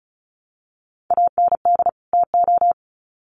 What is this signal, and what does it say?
Morse code, heard as a keyed tone, about 34 wpm, 700 Hz. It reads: AND TO
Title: Sets of 2 Words 35wpm